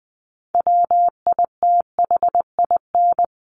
Morse code, heard as a keyed tone, about 20 wpm, 700 Hz. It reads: WITHIN
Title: Top 501-750 Words 20wpm